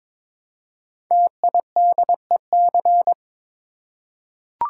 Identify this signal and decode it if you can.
Morse code, heard as a keyed tone, about 22 wpm, 700 Hz. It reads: TIDEC
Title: Code Groups - 5 Letters - 22wpm